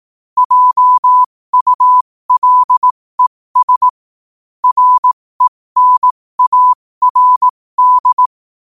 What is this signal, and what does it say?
Morse code, heard as a keyed tone, about 18 wpm, spaced as for 15 wpm, 1000 Hz. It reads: JULES RENARD
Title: 15 WPM morse code quotes for Thu, 29 Feb 2024